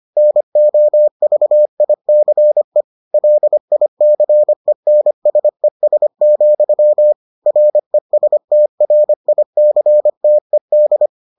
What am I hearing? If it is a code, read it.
Morse code, heard as a keyed tone, about 25 wpm, 600 Hz. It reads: NOVICE LICENSES, RESTRICTED